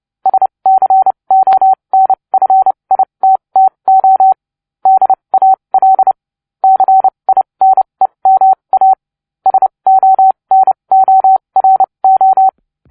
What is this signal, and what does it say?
Morse code, heard as a keyed tone, about 29 wpm, spaced as for 23 wpm, 760 Hz. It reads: SCONFITTO DAL CINEMA SONORO